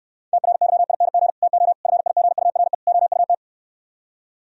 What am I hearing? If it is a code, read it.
Morse code, heard as a keyed tone, about 67 wpm, 700 Hz. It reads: TRYING TO FIGURE OUT